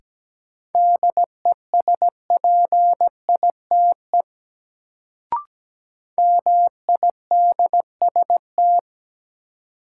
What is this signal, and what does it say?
Morse code, heard as a keyed tone, about 17 wpm, 700 Hz. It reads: DESPITE MIDST
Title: Sets of 4 Words - New Word Drill - 17wpm